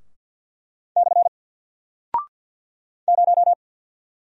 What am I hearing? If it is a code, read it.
Morse code, heard as a keyed tone, about 48 wpm, 700 Hz. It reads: / 0
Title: Single Character (Letter:Number:Punctuation:Prosign) Alt 50wpm